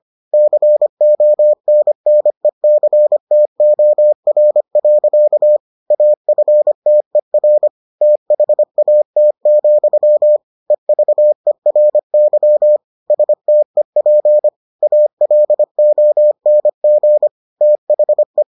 Morse code, heard as a keyed tone, about 25 wpm, 600 Hz. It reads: CONNECTOR. AFTER THAT, EVERY STEP ALONG THE